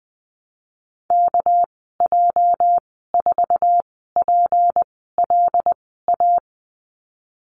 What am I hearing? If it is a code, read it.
Morse code, heard as a keyed tone, about 20 wpm, spaced as for 15 wpm, 700 Hz. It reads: KJ4PLA
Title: 231 Lesson 37g - Letter x - Call Signs 20wpm